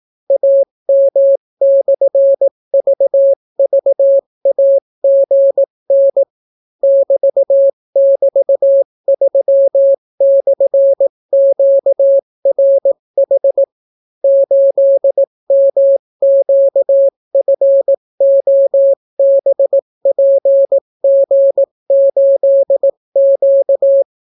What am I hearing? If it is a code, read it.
Morse code, heard as a keyed tone, about 18 wpm, 550 Hz. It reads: AM/VVAGN <BT><BT>3/QRH 8MQFOBPG8Q